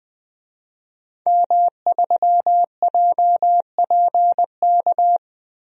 Morse code, heard as a keyed tone, about 20 wpm, 700 Hz. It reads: M3JPK
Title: DXCC-100 Call Signs - 5 Characters - 20wpm